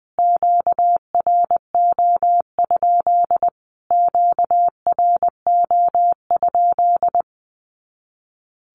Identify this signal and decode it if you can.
Morse code, heard as a keyed tone, about 20 wpm, 700 Hz. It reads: QRO? QRO?
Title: Q-Codes Repeated 3x 20wpm